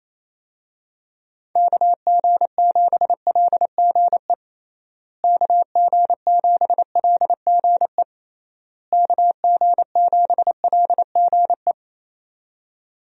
Morse code, heard as a keyed tone, about 28 wpm, 700 Hz. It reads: KG7LGE KG7LGE KG7LGE
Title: USA Call Signs - 6 Characters - Repeated 3x - 28wpm